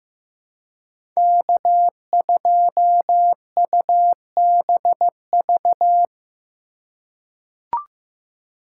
Morse code, heard as a keyed tone, about 15 wpm, 700 Hz. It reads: K2UBV
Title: USA Call Signs - 5 Characters - 2x Character Spacing - 15wpm